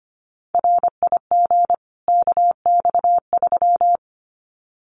Morse code, heard as a keyed tone, about 25 wpm, 700 Hz. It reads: RIG KX3
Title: Common QSO Phrases 25wpm